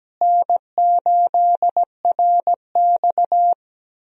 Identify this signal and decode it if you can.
Morse code, heard as a keyed tone, about 17 wpm, 700 Hz. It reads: N8RX
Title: USA Call Signs - 4 Characters - Repeated 3x - 17wpm